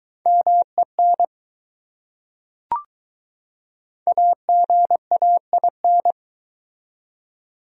Morse code, heard as a keyed tone, about 20 wpm, 700 Hz. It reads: MEN AGAIN